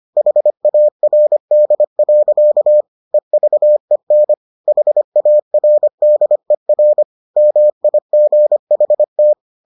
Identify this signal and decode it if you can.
Morse code, heard as a keyed tone, about 25 wpm, 600 Hz. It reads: HARD. EVEN HARDER MIGHT